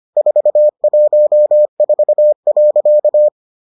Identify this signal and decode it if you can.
Morse code, heard as a keyed tone, about 25 wpm, 600 Hz. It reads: 414.